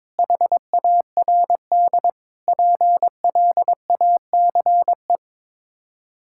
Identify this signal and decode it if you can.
Morse code, heard as a keyed tone, about 23 wpm, 700 Hz. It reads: HARD PLACE
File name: practice / 22wpm / Sets of 7 Words - 22wpm